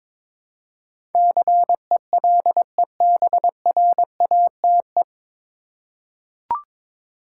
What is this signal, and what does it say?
Morse code, heard as a keyed tone, about 22 wpm, 700 Hz. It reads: CELEBRATE
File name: 9 Letter Words - First 3 Letters Given - 22wpm